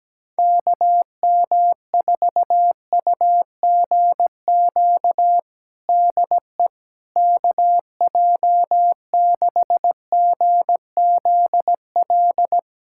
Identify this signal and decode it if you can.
Morse code, heard as a keyed tone, about 17 wpm, 700 Hz. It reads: KM4UGQ DE KJ6GZL
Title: USA Callsign DE Callsign 17wpm